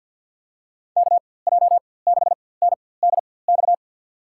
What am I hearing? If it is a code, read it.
Morse code, heard as a keyed tone, about 53 wpm, spaced as for 24 wpm, 700 Hz. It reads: KJ6NDX
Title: USA Call Signs - 6 Characters - 4x Character Spacing - 50wpm